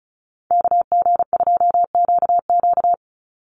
Morse code, heard as a keyed tone, about 34 wpm, 700 Hz. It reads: KG2QQ
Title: USA Call Signs 35wpm